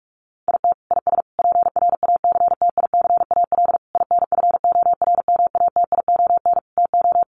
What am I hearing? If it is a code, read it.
Morse code, heard as a keyed tone, about 44 wpm, 700 Hz. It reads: IT IS PRACTICAL INFORMATION TO
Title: Sentences from Top 1400 Words 45wpm